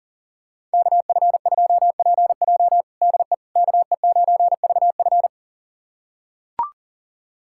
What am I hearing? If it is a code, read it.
Morse code, heard as a keyed tone, about 40 wpm, 700 Hz. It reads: KF2PJ DE KE9VF
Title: USA Callsign DE Callsign - 5 Characters - 40wpm